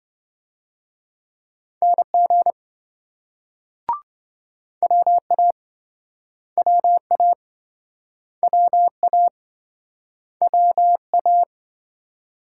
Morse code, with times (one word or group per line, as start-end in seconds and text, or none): 1.8-2.5 NG
4.8-5.5 WA
6.6-7.3 WA
8.4-9.3 WA
10.4-11.4 WA